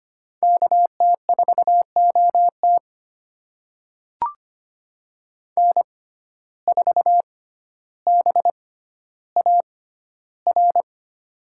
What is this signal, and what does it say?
Morse code, heard as a keyed tone, about 25 wpm, spaced as for 13 wpm, 700 Hz. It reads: KT4OT N4BAR
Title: USA Call Signs - 5 Characters - 6x Character Spacing - 25wpm